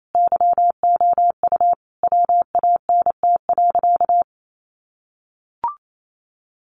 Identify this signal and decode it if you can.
Morse code, heard as a keyed tone, about 28 wpm, 700 Hz. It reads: YOU WANT.
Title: Sentences from Top 1300 Words 28wpm